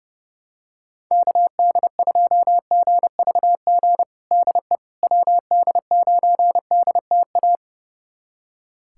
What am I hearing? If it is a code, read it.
Morse code, heard as a keyed tone, about 30 wpm, 700 Hz. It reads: KD2GVG DE WD9DTA
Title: USA Callsign DE Callsign - 6 Characters - 30wpm